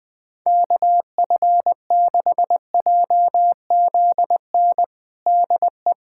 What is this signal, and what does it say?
Morse code, heard as a keyed tone, about 20 wpm, 700 Hz. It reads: KF6JZN DE
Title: USA Callsign DE Callsign - 6 Characters - 20wpm